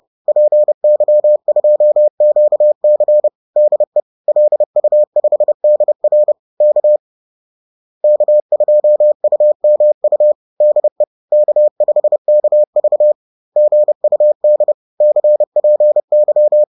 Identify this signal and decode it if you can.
Morse code, heard as a keyed tone, about 30 wpm, 600 Hz. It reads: PY2QC DE LU5DR K K2UMU DE K5KV GUD CPY